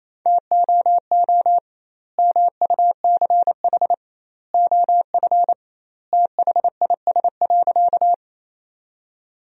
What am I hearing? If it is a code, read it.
Morse code, heard as a keyed tone, about 28 wpm, 700 Hz. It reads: TOO MUCH OF THIS.
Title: 2x Word-Spacing - Sentences from Top 200 Words - 28wpm